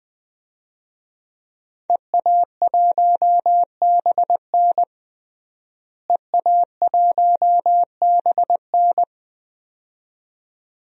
Morse code, heard as a keyed tone, about 20 wpm, 700 Hz. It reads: EA1BN EA1BN
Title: DXCC-100 Call Signs - 5 Characters - Repeated 2x - 20wpm